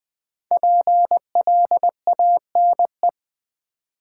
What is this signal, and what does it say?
Morse code, heard as a keyed tone, about 20 wpm, 700 Hz. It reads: PLANE
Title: Sentences from Top 1000 Words - Review Part 3 - New Word Drill - 20wpm